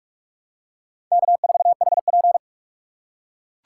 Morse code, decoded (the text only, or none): K4HP